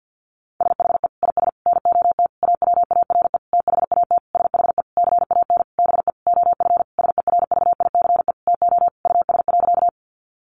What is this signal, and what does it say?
Morse code, heard as a keyed tone, about 50 wpm, 700 Hz. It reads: SHE IS NOT AWARE THAT SHE CAN BE OF SERVICE TO US.
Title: Sentences from Top 1200 Words 50wpm